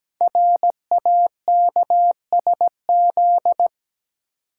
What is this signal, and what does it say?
Morse code, heard as a keyed tone, about 17 wpm, 700 Hz. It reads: RAKSZ
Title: Code Groups - 5 Letter:Numbers - 17wpm